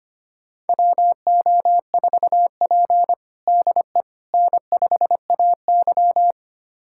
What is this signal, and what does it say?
Morse code, heard as a keyed tone, about 25 wpm, 700 Hz. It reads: WO4P DE N5AY
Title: USA Callsign DE Callsign - 4 Characters - 25wpm